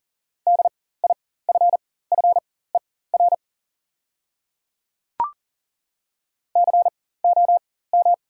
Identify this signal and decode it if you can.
Morse code, heard as a keyed tone, about 40 wpm, spaced as for 19 wpm, 690 Hz. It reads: DIFFER COM
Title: 4x Character Spacing - Top 200 Words - 40wpm